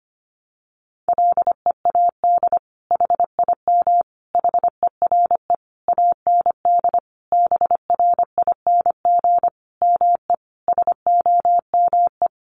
Morse code, heard as a keyed tone, about 25 wpm, 700 Hz. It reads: LEAD HIM HERE AND BRING ME SOME
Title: Sentences from Top 1200 Words 25wpm